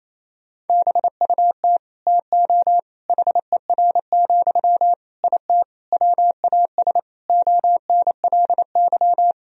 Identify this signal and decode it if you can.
Morse code, heard as a keyed tone, about 28 wpm, 700 Hz. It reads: BUT TO HER, IT WAS ONLY
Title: Sentences from Top 1000 Words - Review Part 2 - 28wpm